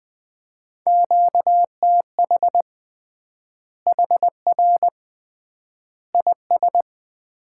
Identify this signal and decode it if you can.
Morse code, heard as a keyed tone, about 20 wpm, 700 Hz. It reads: QTH HR IS
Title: Common QSO Phrases - 3x Word-Spacing - 20wpm